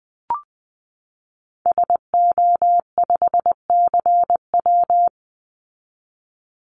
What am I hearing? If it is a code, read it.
Morse code, heard as a keyed tone, about 20 wpm, 700 Hz. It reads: SO5CW
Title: Call Signs Repeated 3x - 20wpm